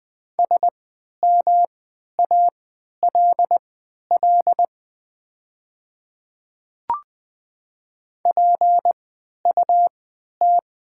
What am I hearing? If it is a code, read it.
Morse code, heard as a keyed tone, about 20 wpm, spaced as for 11 wpm, 690 Hz. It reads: SMALL PUT
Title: 3x Character Spacing - Top 200 Words - 20wpm